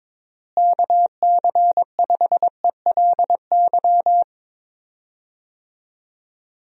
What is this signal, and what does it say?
Morse code, heard as a keyed tone, about 22 wpm, 700 Hz. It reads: KC5ELY
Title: USA Call Signs - 6 Characters - Repeated 2x - 22wpm